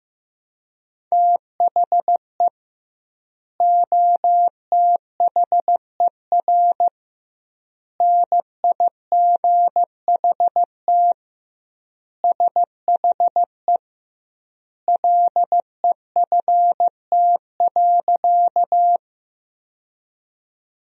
Morse code, heard as a keyed tone, about 15 wpm, 700 Hz. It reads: THE OTHER NIGHT SHE LEFT.
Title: 2x Word-Spacing - Sentences from Top 300 Words - 15wpm